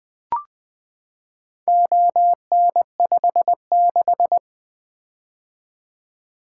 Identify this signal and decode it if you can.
Morse code, heard as a keyed tone, about 20 wpm, 700 Hz. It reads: ON56
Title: Four Letter-Number Combinations - 20wpm